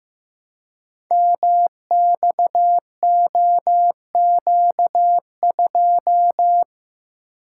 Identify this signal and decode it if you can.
Morse code, heard as a keyed tone, about 15 wpm, 700 Hz. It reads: MXOQ2
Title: Code Groups - 5 Letter:Numbers - 15wpm